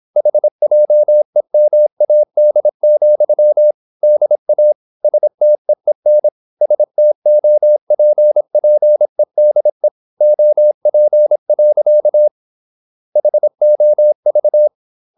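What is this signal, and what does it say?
Morse code, heard as a keyed tone, about 26 wpm, 600 Hz. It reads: HJEMAD, DA STEEN STOPPEDE OP. HOV
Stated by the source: High-speed (QRQ) Morse at 26 wpm.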